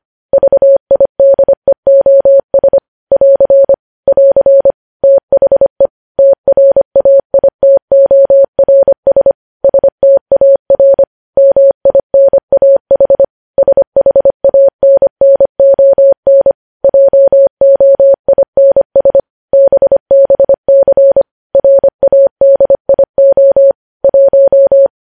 Morse code, heard as a keyed tone, about 25 wpm, 550 Hz. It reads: VIDEOS <AR> <AR> THE TRAITORS STAR MINAH SHANNON JOINS BBC RADIO 1
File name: podcast / News headlines-1x-25x25